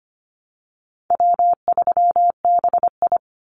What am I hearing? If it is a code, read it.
Morse code, heard as a keyed tone, about 25 wpm, 700 Hz. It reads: W3BI
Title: Easy Call Signs Repeated 3x - 25wpm